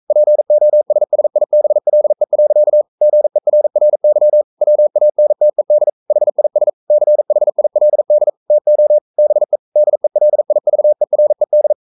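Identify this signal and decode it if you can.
Morse code, heard as a keyed tone, about 42 wpm, 600 Hz. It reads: POSSIBLE. GERRY WANTED HIS CHILD TO BE DELIVERED
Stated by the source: Morse code sending of an English novel at 42 wpm.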